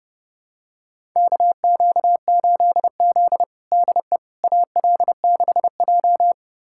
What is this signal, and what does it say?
Morse code, heard as a keyed tone, about 30 wpm, 700 Hz. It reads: KQ8Z DE AL6J
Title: USA Callsign DE Callsign - 4 Characters - 30wpm